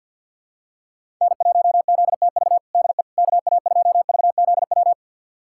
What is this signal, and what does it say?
Morse code, heard as a keyed tone, about 46 wpm, 690 Hz. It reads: N1ZTV DE KA2VZW
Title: USA Callsign DE Callsign 50wpm